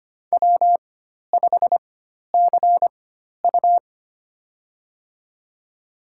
Morse code, heard as a keyed tone, about 25 wpm, spaced as for 12 wpm, 700 Hz. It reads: W5CU
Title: USA Call Signs - 4 Characters - 4x Character Spacing - 25wpm